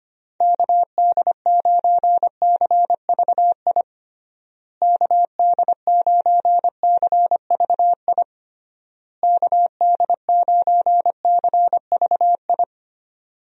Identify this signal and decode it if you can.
Morse code, heard as a keyed tone, about 25 wpm, 700 Hz. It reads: KD9CVI KD9CVI KD9CVI
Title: USA Call Signs - 6 Characters - Repeated 3x - 25wpm